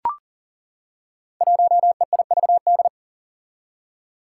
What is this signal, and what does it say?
Morse code, heard as a keyed tone, about 40 wpm, 700 Hz. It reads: 1EIVD
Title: Five Challenging Letter-Number Combinations 40wpm